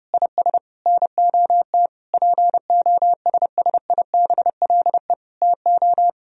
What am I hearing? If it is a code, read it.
Morse code, heard as a keyed tone, about 32 wpm, 700 Hz. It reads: IS NOT POSSIBLE TO
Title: Sentences from Top 1000 Words - Review Part 4 - 30wpm